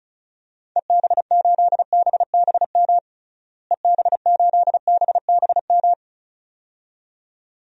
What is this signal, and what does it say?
Morse code, heard as a keyed tone, about 36 wpm, 700 Hz. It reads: EB8BBM EB8BBM
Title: DXCC-100 Call Signs - 6 Characters - Repeated 2x - 35wpm